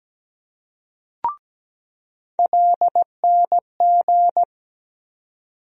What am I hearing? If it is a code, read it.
Morse code, heard as a keyed tone, about 17 wpm, 700 Hz. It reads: LNG
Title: USA Call Sign Postfixes - 3 Letters 17wpm